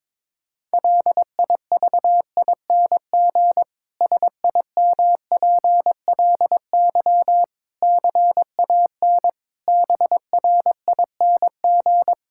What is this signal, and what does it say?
Morse code, heard as a keyed tone, about 22 wpm, 700 Hz. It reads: LIVING SIMPLY CAN BRING
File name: Difficult Sentences with Hints - 22wpm